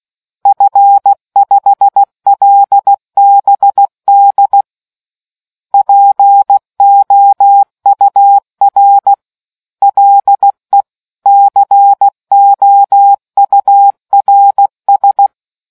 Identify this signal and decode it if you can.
Morse code, heard as a keyed tone, about 16 wpm, 800 Hz. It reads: F5LBD POUR LE COURS